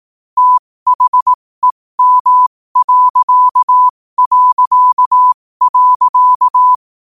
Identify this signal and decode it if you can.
Morse code, heard as a keyed tone, about 18 wpm, spaced as for 15 wpm, 1000 Hz. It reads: THEM...
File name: qotdfiles / 15wpm